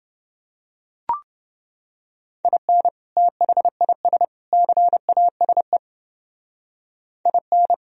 Morse code, heard as a keyed tone, about 30 wpm, 700 Hz. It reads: IN THIS CASE IN